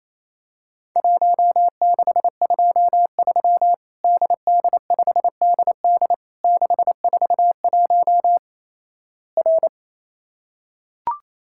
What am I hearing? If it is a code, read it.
Morse code, heard as a keyed tone, about 27 wpm, 700 Hz. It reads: 1623 DD5DD 641 R
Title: Worked All Europe DX Contest - QTC Exchange - 28wpm